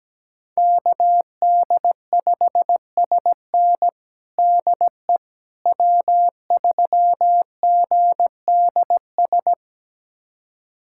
Morse code, heard as a keyed tone, about 17 wpm, 700 Hz. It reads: KD5SN DE W3GDS
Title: USA Callsign DE Callsign 17wpm